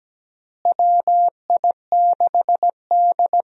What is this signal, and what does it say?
Morse code, heard as a keyed tone, about 17 wpm, 700 Hz. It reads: WI6D
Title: USA Call Signs - 4 Characters - Repeated 3x - 17wpm